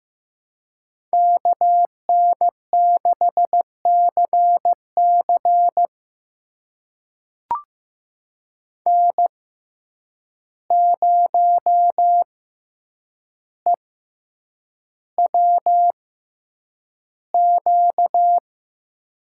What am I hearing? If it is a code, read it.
Morse code, heard as a keyed tone, about 15 wpm, spaced as for 8 wpm, 700 Hz. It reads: KN6CC N0EWQ
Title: USA Call Signs - 5 Characters - 6x Character Spacing - 15wpm